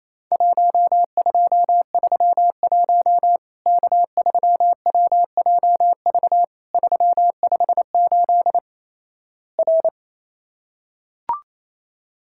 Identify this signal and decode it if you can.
Morse code, heard as a keyed tone, about 28 wpm, 700 Hz. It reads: 1231 K3WJV 358 R
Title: Worked All Europe DX Contest - QTC Exchange - 28wpm